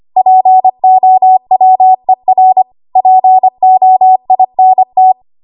Morse code, heard as a keyed tone, about 25 wpm, 750 Hz. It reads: POWER POINT
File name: Archive / 25 WPM / 25WPM